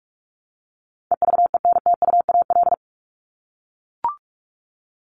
Morse code, heard as a keyed tone, about 44 wpm, 700 Hz. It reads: EVENTUAL
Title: LY Suffix - Top 100 Words with Base Words 45wpm